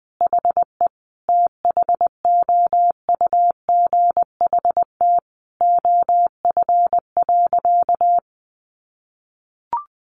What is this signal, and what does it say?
Morse code, heard as a keyed tone, about 20 wpm, 700 Hz. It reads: HE THOUGHT OF.
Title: Sentences from Top 300 Words 20wpm